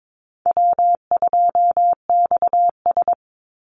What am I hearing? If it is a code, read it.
Morse code, heard as a keyed tone, about 22 wpm, 700 Hz. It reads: W2XS
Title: Easy Call Signs Repeated 2x - 22wpm